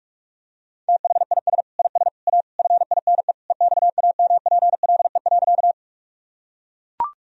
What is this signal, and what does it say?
Morse code, heard as a keyed tone, about 42 wpm, 700 Hz. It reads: THIS IS A FINE EXAMPLE.
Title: Sentences from Top 1000 Words - Review Part 1 - 45wpm